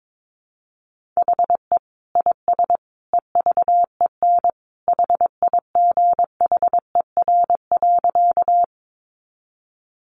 Morse code, heard as a keyed tone, about 22 wpm, 700 Hz. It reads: HE IS EVEN HIGHER.
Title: Additional Easy Phrases 22wpm